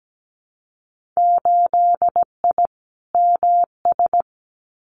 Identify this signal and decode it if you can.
Morse code, heard as a keyed tone, about 17 wpm, 700 Hz. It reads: 8I MS
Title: Winter Field Day Exchanges Encore 17wpm